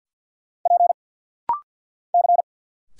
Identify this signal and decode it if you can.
Morse code, heard as a keyed tone, about 50 wpm, 700 Hz. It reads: P C
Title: Single Letter-Number Alt - Rapid-Fire 50wpm